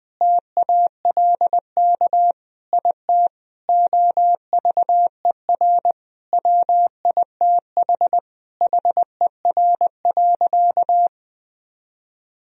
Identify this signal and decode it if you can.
Morse code, heard as a keyed tone, about 20 wpm, 700 Hz. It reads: TALK IT OVER WITH HER.
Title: Sentences from Top 1000 Words - Review Part 2 - 20wpm